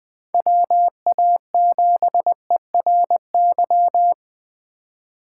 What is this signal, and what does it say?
Morse code, heard as a keyed tone, about 24 wpm, 700 Hz. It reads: WA7ERY
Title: Speed Racing - USA 2x3 Call Signs 20wpm